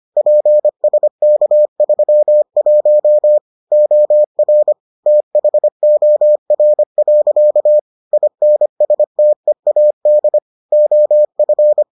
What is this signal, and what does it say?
Morse code, heard as a keyed tone, about 25 wpm, 600 Hz. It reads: PSK31 OR THOR. INSTEAD OF